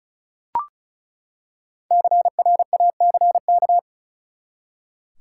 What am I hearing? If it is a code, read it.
Morse code, heard as a keyed tone, about 36 wpm, 700 Hz. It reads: CRACK
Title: Five Letter Words - 35wpm